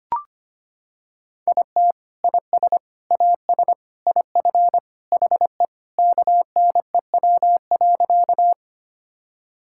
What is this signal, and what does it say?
Morse code, heard as a keyed tone, about 25 wpm, 700 Hz. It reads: IT IS AS IF HE KNEW.
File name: practice / 25wpm / Sentences with 3-Word Phrases - 25wpm